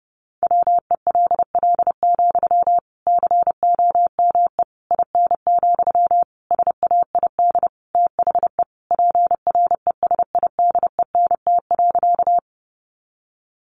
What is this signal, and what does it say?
Morse code, heard as a keyed tone, about 30 wpm, 700 Hz. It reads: WELL, COME IN, SAID THE PRESIDENT.
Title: Sentences from Top 1000 Words 30wpm